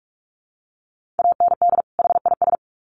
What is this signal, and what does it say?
Morse code, heard as a keyed tone, about 46 wpm, 700 Hz. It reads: AND HIS
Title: Sets of 2 Words 45wpm